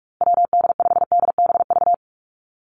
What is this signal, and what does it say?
Morse code, heard as a keyed tone, about 44 wpm, 700 Hz. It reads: WD5DBV
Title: Call Signs Repeated 2x - 45wpm